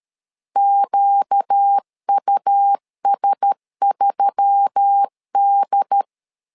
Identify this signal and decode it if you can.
Morse code, heard as a keyed tone, about 13 wpm, 800 Hz. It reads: QUS3D
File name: LEZIONI / 5 LBVC / 5 LBV 12wpm 1855